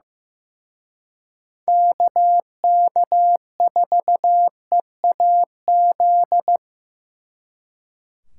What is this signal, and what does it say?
Morse code, heard as a keyed tone, about 18 wpm, 700 Hz. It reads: KK4EAZ
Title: Speed Racing - USA Call Signs 15wpm